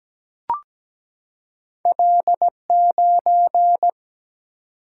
Morse code, heard as a keyed tone, about 17 wpm, 700 Hz. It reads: L9
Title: Single-Letter Single-Number 17wpm